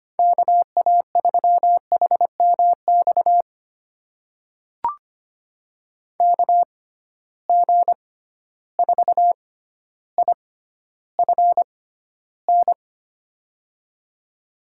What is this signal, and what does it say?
Morse code, heard as a keyed tone, about 25 wpm, spaced as for 13 wpm, 700 Hz. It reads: KA3HMX KG4IFN
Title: USA Call Signs - 6 Characters - 6x Character Spacing - 25wpm